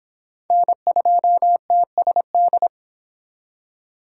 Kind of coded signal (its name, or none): Morse code